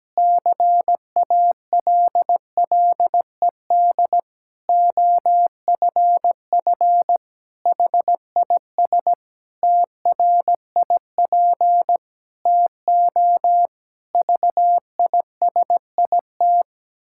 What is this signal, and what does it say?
Morse code, heard as a keyed tone, about 17 wpm, 700 Hz. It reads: CALLED OFF HIS TRIP TO VISIT
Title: Pangram Challenge v2 - 17wpm